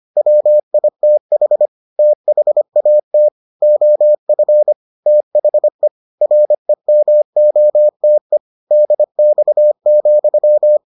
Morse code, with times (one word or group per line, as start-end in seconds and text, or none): 0.2-1.7 WITH
2.0-3.3 THAT
3.6-4.7 OF
5.1-5.9 THE
6.2-8.4 REMOTE
8.7-10.8 DX,